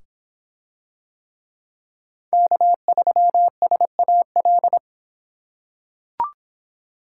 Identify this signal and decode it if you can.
Morse code, heard as a keyed tone, about 24 wpm, 700 Hz. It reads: K3SAL